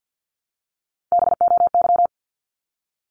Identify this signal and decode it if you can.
Morse code, heard as a keyed tone, about 48 wpm, 700 Hz. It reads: BOY
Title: Top 200 Words 50wpm